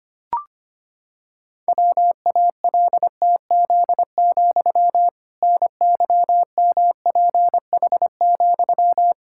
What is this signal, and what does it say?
Morse code, heard as a keyed tone, about 25 wpm, 700 Hz. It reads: WALTZ, NYMPH,
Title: Pangram Challenge v2 - 25wpm